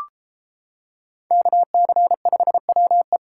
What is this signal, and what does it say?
Morse code, heard as a keyed tone, about 32 wpm, 700 Hz. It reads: KC5WE